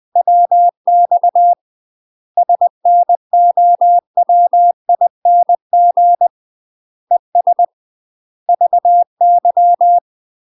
Morse code, heard as a keyed tone, about 20 wpm, 700 Hz. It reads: WX SNOWING ES VY
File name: UK MASTER QSOs TRAINING / 20WPM / UK MASTER QSOs 0068